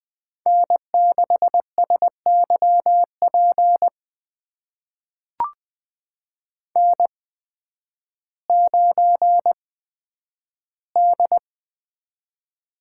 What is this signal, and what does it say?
Morse code, heard as a keyed tone, about 20 wpm, spaced as for 11 wpm, 700 Hz. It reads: N6SYP N9D
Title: USA Call Signs - 8x Character Spacing - 20wpm